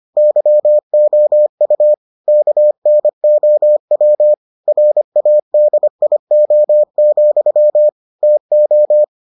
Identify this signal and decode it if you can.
Morse code, heard as a keyed tone, about 25 wpm, 600 Hz. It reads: YOU KNOW RADIO, TO